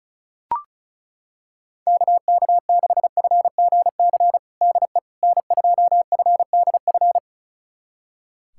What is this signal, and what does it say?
Morse code, heard as a keyed tone, about 36 wpm, 690 Hz. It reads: KK6FGC DE N2FDF
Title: USA Callsign DE Callsign 35wpm